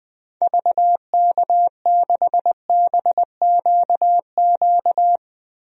Morse code, heard as a keyed tone, about 20 wpm, 700 Hz. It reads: VK6BQQ
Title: DXCC-100 Call Signs - 6 Characters - Repeated 2x - 20wpm